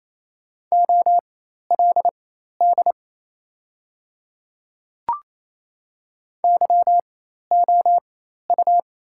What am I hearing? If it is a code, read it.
Morse code, heard as a keyed tone, about 28 wpm, spaced as for 13 wpm, 690 Hz. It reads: OLD YOU
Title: 4x Character Spacing - Top 200 Words - 28wpm